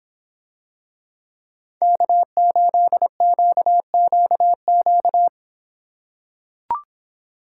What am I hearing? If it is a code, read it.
Morse code, heard as a keyed tone, about 24 wpm, 700 Hz. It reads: K8QQQ